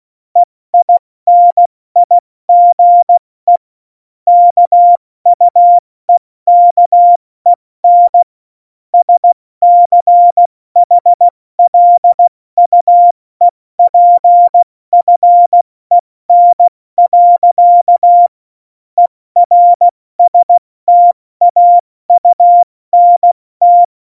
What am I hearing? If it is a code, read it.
Morse code, heard as a keyed tone, about 16 wpm, 700 Hz. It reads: EINIGE KUEKEN SCHLUEPFEN. ERSTAUNT
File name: Morsekurs / Witze / Witz 01. August 17, Tempo 70 F80